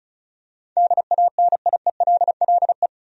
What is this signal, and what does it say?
Morse code, heard as a keyed tone, about 34 wpm, 700 Hz. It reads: DANIELLE
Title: USA Common Names - 35wpm